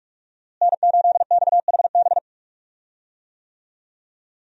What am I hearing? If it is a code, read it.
Morse code, heard as a keyed tone, about 46 wpm, 700 Hz. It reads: N8XHB
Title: USA Call Signs - 5 Characters - Repeated 2x - 45wpm